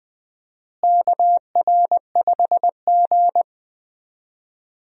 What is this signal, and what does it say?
Morse code, heard as a keyed tone, about 21 wpm, 700 Hz. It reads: KR5G
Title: USA Call Signs - 4 Characters - Repeated 2x - 20wpm